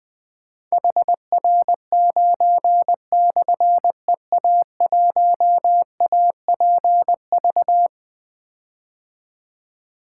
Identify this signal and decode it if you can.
Morse code, heard as a keyed tone, about 20 wpm, 700 Hz. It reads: HR9/EA1APV
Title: Difficult Call Signs 20wpm